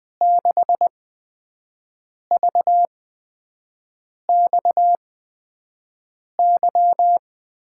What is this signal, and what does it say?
Morse code, heard as a keyed tone, about 20 wpm, spaced as for 5 wpm, 700 Hz. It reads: 6VXY
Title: USA Call Signs - 8x Character Spacing - 20wpm